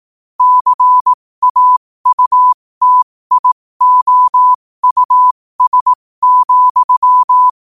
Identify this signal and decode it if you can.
Morse code, heard as a keyed tone, about 18 wpm, spaced as for 15 wpm, 1000 Hz. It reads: CAUTIOUS,
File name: qotdfiles / 15wpm